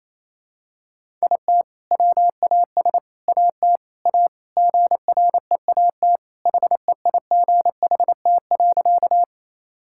Morse code, heard as a keyed tone, about 28 wpm, 700 Hz. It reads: IT WAS AT A GREAT HEIGHT.
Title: Sentences from Top 1600 Words 28wpm